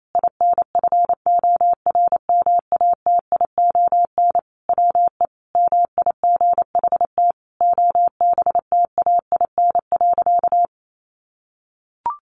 Morse code, heard as a keyed tone, about 28 wpm, 700 Hz. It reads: INFORMATION WE MIGHT OBTAIN.
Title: Sentences from Top 1300 Words 28wpm